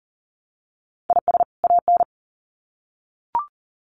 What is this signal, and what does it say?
Morse code, heard as a keyed tone, about 40 wpm, 700 Hz. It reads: IS AN
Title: Sets of 2 Words 40wpm